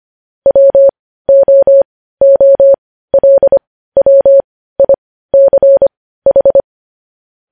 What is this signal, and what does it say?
Morse code, heard as a keyed tone, about 25 wpm, spaced as for 15 wpm, 550 Hz. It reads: WOOLWICH
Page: News Headlines in Morse Code at 15 WPM